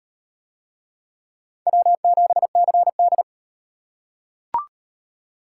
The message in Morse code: W7CD

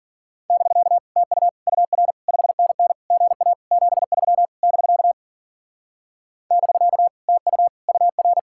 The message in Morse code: <BK> TU UR 5NN GA 73 <BK> <BK> TU UR

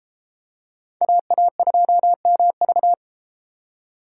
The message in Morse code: AA2MV

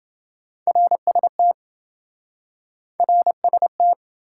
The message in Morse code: RST RST

